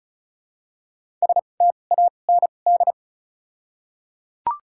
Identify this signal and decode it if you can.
Morse code, heard as a keyed tone, about 34 wpm, spaced as for 25 wpm, 700 Hz. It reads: STAND
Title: 2x Character Spacing - Top 500 Words - 35wpm